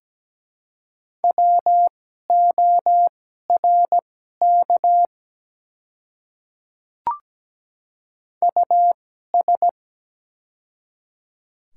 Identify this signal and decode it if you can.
Morse code, heard as a keyed tone, about 17 wpm, spaced as for 12 wpm, 700 Hz. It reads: WORK US
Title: 2x Character Spacing - Top 200 Words - 17wpm